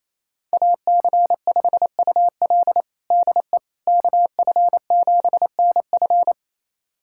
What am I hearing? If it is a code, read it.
Morse code, heard as a keyed tone, about 28 wpm, 700 Hz. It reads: AC5UL DE KF7NF
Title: USA Callsign DE Callsign 28wpm